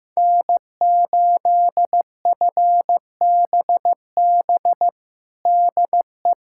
Morse code, heard as a keyed tone, about 15 wpm, 700 Hz. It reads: N8FBB DE
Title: USA Callsign DE Callsign - 5 Characters - 15wpm